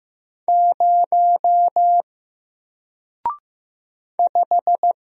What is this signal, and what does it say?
Morse code, heard as a keyed tone, about 15 wpm, 700 Hz. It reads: 0 5
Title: Single Letter-Number Alt 15wpm